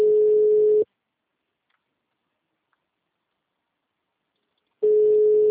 Ringback tone (Europe, 425 Hz).